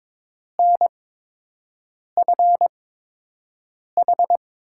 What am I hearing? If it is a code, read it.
Morse code, heard as a keyed tone, about 22 wpm, spaced as for 6 wpm, 700 Hz. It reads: NFH
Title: USA Call Signs - 6 Characters - 8x Character Spacing - 22wpm